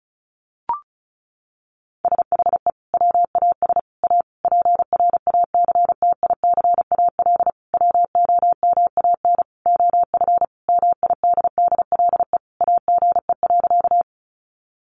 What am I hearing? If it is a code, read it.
Morse code, heard as a keyed tone, about 34 wpm, 700 Hz. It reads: SHE WAS A PRACTICAL WOMAN OF MIDDLE AGE.
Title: Sentences from Top 1400 Words 35wpm